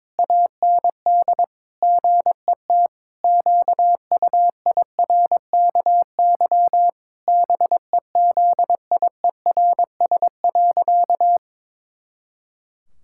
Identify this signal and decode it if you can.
Morse code, heard as a keyed tone, about 22 wpm, 700 Hz. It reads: AND GET QUIRKY BEZIERS.
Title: Pangram Challenge v2 - 22wpm